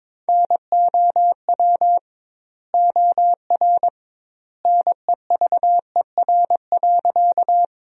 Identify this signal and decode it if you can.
Morse code, heard as a keyed tone, about 22 wpm, 700 Hz. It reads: NOW OR NEVER.
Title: 2x Word-Spacing - Sentences from Top 300 Words - 22wpm